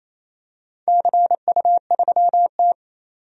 Morse code, heard as a keyed tone, about 28 wpm, 700 Hz. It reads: CU3T
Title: DXCC-100 Call Signs - 4 Characters - 28wpm